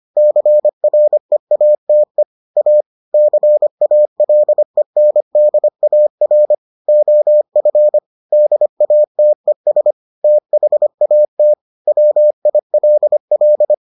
Morse code, heard as a keyed tone, about 25 wpm, 600 Hz. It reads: CREATE A CALENDAR OF DATES THAT WILL